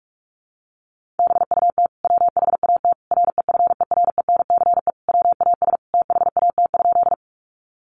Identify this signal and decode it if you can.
Morse code, heard as a keyed tone, about 44 wpm, 700 Hz. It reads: BUT WHAT REFERENCE WAS THAT?
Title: Sentences from Top 1200 Words 45wpm